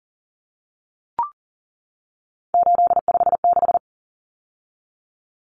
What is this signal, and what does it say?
Morse code, heard as a keyed tone, about 37 wpm, 700 Hz. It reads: 856
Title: Numbers - 3-Digit 40wpm